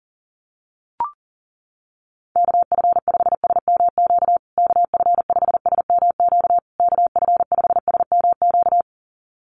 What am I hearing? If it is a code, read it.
Morse code, heard as a keyed tone, about 40 wpm, 700 Hz. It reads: KF5SMQ KF5SMQ KF5SMQ
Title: USA Call Signs Repeated 3x 40wpm